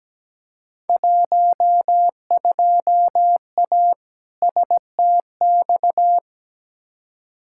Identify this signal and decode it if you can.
Morse code, heard as a keyed tone, about 17 wpm, 700 Hz. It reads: 12A STX
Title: Field Day Exchanges Encore - 17wpm